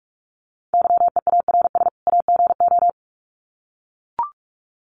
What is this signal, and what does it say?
Morse code, heard as a keyed tone, about 44 wpm, 700 Hz. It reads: YEARS AGO
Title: Sets of 2 Words 45wpm